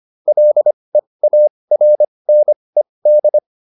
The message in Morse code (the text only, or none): LEARNED